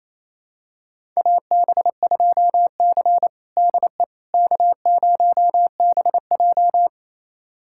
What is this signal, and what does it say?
Morse code, heard as a keyed tone, about 28 wpm, 700 Hz. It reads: AB2C DE K0BJ
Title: USA Callsign DE Callsign - 4 Characters - 28wpm